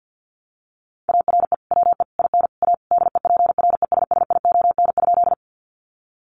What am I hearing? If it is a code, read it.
Morse code, heard as a keyed tone, about 50 wpm, 700 Hz. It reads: ARE WE IN A DEPRESSION?
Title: Sentences from Top 1300 Words 50wpm